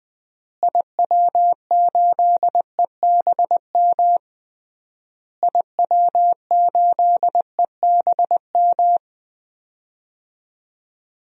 Morse code, heard as a keyed tone, about 20 wpm, 700 Hz. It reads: IW8EBM IW8EBM
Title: DXCC-100 Call Signs - 6 Characters - Repeated 3x - 20wpm